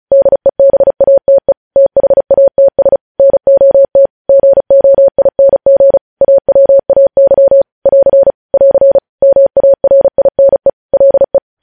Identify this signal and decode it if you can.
Morse code, heard as a keyed tone, about 36 wpm, 550 Hz. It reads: DEBATE THATS NOT GOING AWAY <AR> <AR> MARINE LE
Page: News headlines in Morse code at 35 wpm